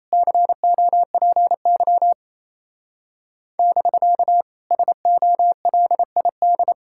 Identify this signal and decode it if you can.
Morse code, heard as a keyed tone, about 37 wpm, 700 Hz. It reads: COPY <BK> SOLID